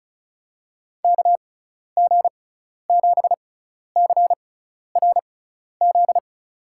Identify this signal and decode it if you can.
Morse code, heard as a keyed tone, about 34 wpm, spaced as for 12 wpm, 700 Hz. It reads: KG7CRZ
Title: USA Call Signs - 6x Character Spacing - 35wpm